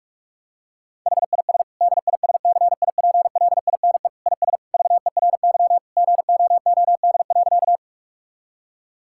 Morse code, heard as a keyed tone, about 46 wpm, 700 Hz. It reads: HIS DISCIPLINE IS VERY GOOD.
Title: Sentences from Top 1000 Words - Review Part 6 - 45wpm